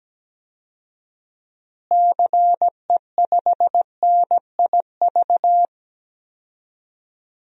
Morse code, heard as a keyed tone, about 17 wpm, 700 Hz. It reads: CE5NIV
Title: DXCC-100 Call Signs - 6 Characters - Repeated 3x - 17wpm